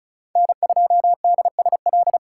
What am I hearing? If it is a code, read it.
Morse code, heard as a keyed tone, about 34 wpm, 700 Hz. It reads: N2DSL
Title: USA Call Signs - 5 Characters - 3x Character Spacing - 35wpm